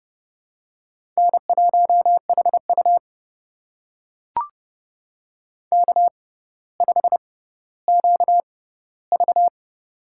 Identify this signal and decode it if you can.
Morse code, heard as a keyed tone, about 30 wpm, spaced as for 15 wpm, 700 Hz. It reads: N1HU K5QV
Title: USA Call Signs - 4 Characters - 6x Character Spacing - 30wpm